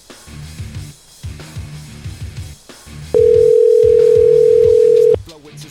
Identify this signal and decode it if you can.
Ringback tone (North America), 440 + 480 Hz.